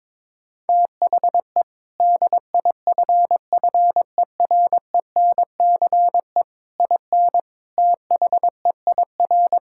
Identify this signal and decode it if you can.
Morse code, heard as a keyed tone, about 22 wpm, 700 Hz. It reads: THE DIFFERENCE IN THEIR
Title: Sentences from Top 1600 Words 22wpm